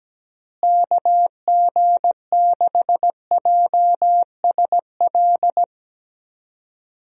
Morse code, heard as a keyed tone, about 17 wpm, 700 Hz. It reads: KG6JSL
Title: USA Call Signs - 6 Characters - Repeated 3x - 17wpm